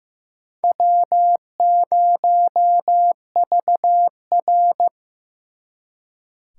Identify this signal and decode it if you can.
Morse code, heard as a keyed tone, about 18 wpm, 700 Hz. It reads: W0VR
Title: Speed Racing - USA 1x2 Call Signs 15wpm